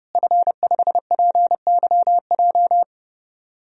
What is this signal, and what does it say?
Morse code, heard as a keyed tone, about 31 wpm, 700 Hz. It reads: F5PYJ
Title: DXCC-100 Call Signs - 5 Characters - Repeated 2x - 30wpm